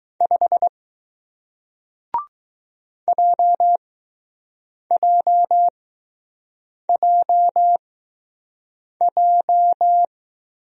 Morse code, heard as a keyed tone, about 20 wpm, spaced as for 6 wpm, 700 Hz. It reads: 5 JJJJ